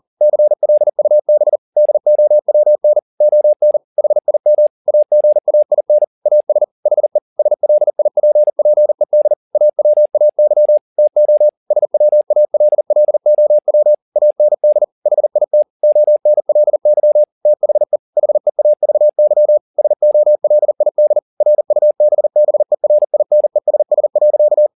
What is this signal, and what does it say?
Morse code, heard as a keyed tone, about 40 wpm, 600 Hz. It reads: CLUB DOWN ON HIM AGAIN AS HE SLIPPED AWAY TO SWALLOW AND HIT ONLY THE HEAVY SOLID RUBBERINESS.
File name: the old man and the sea 2 40wpm